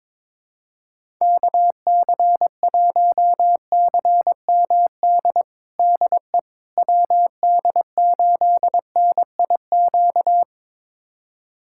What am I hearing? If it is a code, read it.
Morse code, heard as a keyed tone, about 22 wpm, 700 Hz. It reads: KC1CMD DE WD8NIQ
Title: USA Callsign DE Callsign 22wpm